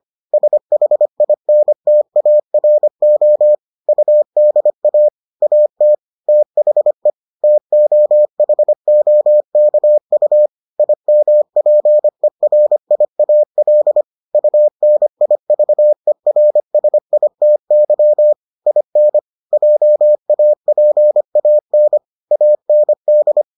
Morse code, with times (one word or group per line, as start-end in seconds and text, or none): 0.3-3.5 SHINTARO
3.9-5.1 UDA
5.4-5.9 AT
6.3-7.1 THE
7.4-10.5 TOHOKU
10.8-14.0 IMPERIAL
14.3-18.3 UNIVERSITY
18.7-19.2 IN
19.5-22.0 JAPAN
22.3-23.4 AND